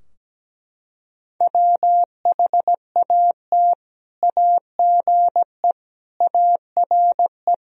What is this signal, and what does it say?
Morse code, heard as a keyed tone, about 17 wpm, 700 Hz. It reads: WHAT AGE ARE
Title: Sentences from Top 1000 Words - Review Part 3 - 17wpm